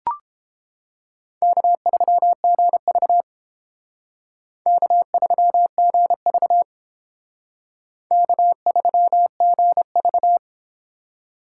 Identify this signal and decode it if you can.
Morse code, heard as a keyed tone, about 31 wpm, 700 Hz. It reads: K3GV K3GV K3GV